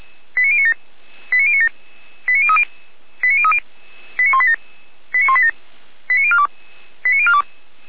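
Five-tone selective call (ZVEI), heard as quick runs of five tones, tones of about 70 ms. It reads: 89097 89097 89030 89030 89187 89187 89042 89042
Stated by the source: ZVEI five-tone call received on SDR.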